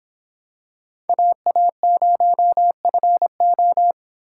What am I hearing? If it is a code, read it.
Morse code, heard as a keyed tone, about 26 wpm, 700 Hz. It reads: AA0FO